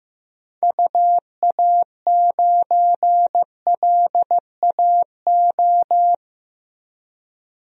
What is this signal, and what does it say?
Morse code, heard as a keyed tone, about 15 wpm, 700 Hz. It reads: UA9LAO
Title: DXCC-100 Call Signs - 6 Characters - Repeated 2x - 15wpm